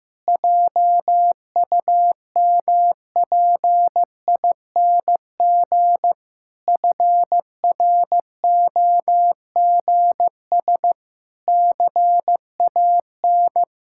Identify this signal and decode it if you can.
Morse code, heard as a keyed tone, about 15 wpm, 700 Hz. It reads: JUMPING FROGS CAN
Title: Pangram Challenge v2 - 15wpm